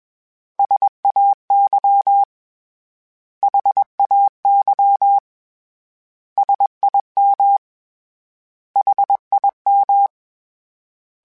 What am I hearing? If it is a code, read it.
Morse code, heard as a keyed tone, about 21 wpm, 800 Hz. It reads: SAY HAY SIM HIM
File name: S vs H / S vs H mp3s / S vs H @20